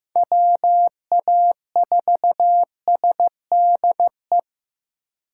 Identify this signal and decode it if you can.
Morse code, heard as a keyed tone, about 15 wpm, 700 Hz. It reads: WA4SDE
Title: USA Call Signs - 6 Characters - 2x Character Spacing - 15wpm